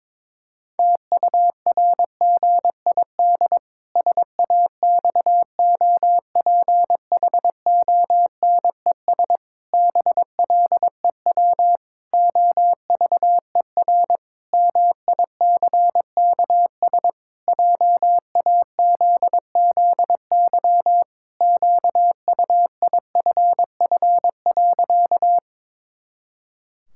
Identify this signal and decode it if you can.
Morse code, heard as a keyed tone, about 22 wpm, 700 Hz. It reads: TURGID SAXOPHONES BLEW OVER MICKS JAZZY QUIFF.
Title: Pangram Challenge v2 - 22wpm